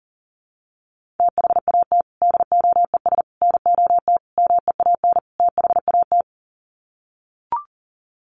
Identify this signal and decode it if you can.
Morse code, heard as a keyed tone, about 40 wpm, 700 Hz. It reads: THAT DOES NOT MEAN THAT
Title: Sets of 5 Words Encore 40wpm